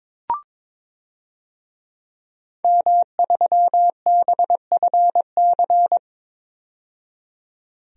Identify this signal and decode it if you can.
Morse code, heard as a keyed tone, about 21 wpm, 700 Hz. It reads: M3BFC
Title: DXCC-100 Call Signs - 5 Characters - 22wpm